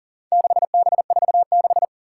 Morse code, heard as a keyed tone, about 42 wpm, 700 Hz. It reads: 6B46
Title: Four Challenging Letter-Number Combinations 40wpm